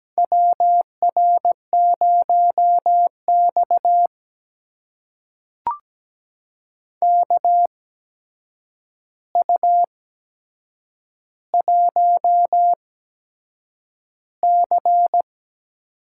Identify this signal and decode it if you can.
Morse code, heard as a keyed tone, about 17 wpm, spaced as for 7 wpm, 700 Hz. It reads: WR0X KU1C
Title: USA Call Signs - 4 Characters - 8x Character Spacing - 17wpm